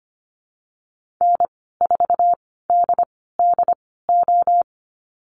Morse code, heard as a keyed tone, about 25 wpm, spaced as for 16 wpm, 700 Hz. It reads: N4DDO
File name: Lesson 15g - Number 5 - Call Signs 25wpm